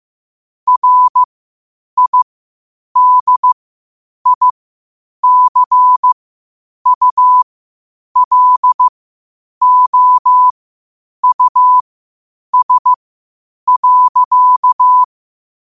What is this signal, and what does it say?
Morse code, heard as a keyed tone, about 15 wpm, spaced as for 8 wpm, 1000 Hz. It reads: RIDICULOUS.
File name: qotdfiles / 8wpm